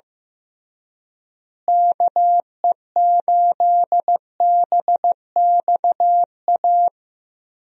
Morse code, heard as a keyed tone, about 16 wpm, 700 Hz. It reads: KE8BXA
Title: USA Call Signs - 6 Characters - Repeated 3x - 15wpm